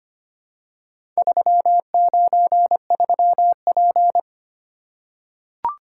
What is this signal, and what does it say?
Morse code, heard as a keyed tone, about 25 wpm, 700 Hz. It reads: 393P
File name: practice / 25wpm / Four Letter-Number Combinations - 25wpm